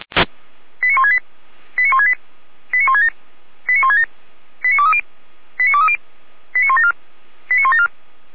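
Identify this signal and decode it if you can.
Five-tone selective call (ZVEI), heard as quick runs of five tones, tones of about 80 ms. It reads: 89168 89168 89167 89167 89230 89230 89174 89174
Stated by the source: ZVEI five-tone selective call 89115 received by SDR.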